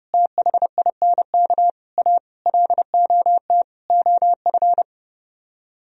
Morse code, heard as a keyed tone, about 36 wpm, 700 Hz. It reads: THINK A LOT OF
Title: Speed-Racing - Sets of 5 Words 30wpm